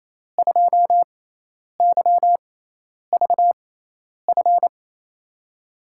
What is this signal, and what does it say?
Morse code, heard as a keyed tone, about 28 wpm, spaced as for 10 wpm, 700 Hz. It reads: 2YVF
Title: USA Call Signs - 6 Characters - 6x Character Spacing - 28wpm